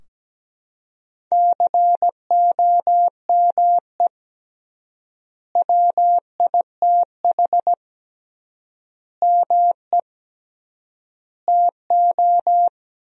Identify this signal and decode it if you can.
Morse code, heard as a keyed tone, about 17 wpm, 700 Hz. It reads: COME WITH ME TO
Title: 3x Word-Spacing - Sentences from Top 200 Words - 17wpm